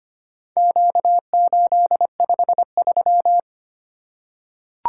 Morse code, heard as a keyed tone, about 25 wpm, 700 Hz. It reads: Q853
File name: practice / 25wpm / Four Letter-Number Combinations - 25wpm